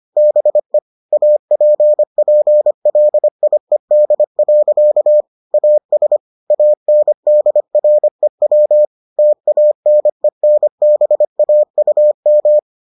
Morse code, heard as a keyed tone, about 25 wpm, 600 Hz. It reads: BE APPLIED. AS ANDREW TANENBAUM